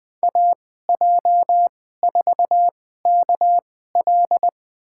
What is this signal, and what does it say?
Morse code, heard as a keyed tone, about 21 wpm, spaced as for 15 wpm, 700 Hz. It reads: AJ4KL
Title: USA Call Signs - 5 Characters - 2x Character Spacing - 20wpm